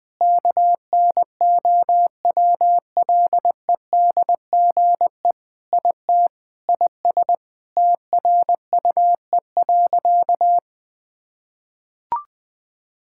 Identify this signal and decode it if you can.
Morse code, heard as a keyed tone, about 21 wpm, 700 Hz. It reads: KNOWLEDGE IT IS TRUE.
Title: Sentences with 6-Word Phrases - 20wpm